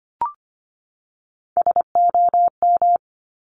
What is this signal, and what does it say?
Morse code, heard as a keyed tone, about 25 wpm, 700 Hz. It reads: SOM
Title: Top Three Letter Combinations in Words 25wpm